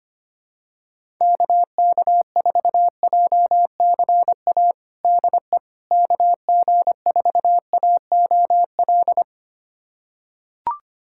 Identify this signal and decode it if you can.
Morse code, heard as a keyed tone, about 25 wpm, 700 Hz. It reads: KK4JCA DE KG4AOL
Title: USA Callsign DE Callsign - 6 Characters - 25wpm